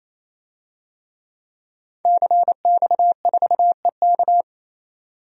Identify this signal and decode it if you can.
Morse code, heard as a keyed tone, about 28 wpm, 700 Hz. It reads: CX4EK
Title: DXCC-100 Call Signs - 5 Characters - 28wpm